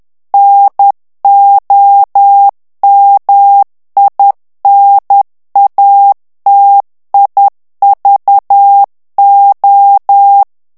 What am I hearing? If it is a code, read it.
Morse code, heard as a keyed tone, about 10 wpm, 800 Hz. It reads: NOMINATIVO
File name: ITALIEN texte clair 2 vitesse 10 mots minute